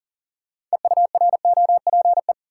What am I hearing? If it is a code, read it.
Morse code, heard as a keyed tone, about 40 wpm, 700 Hz. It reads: EUROPE
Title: Sets of 5 Words Encore - New Word Drill - 40wpm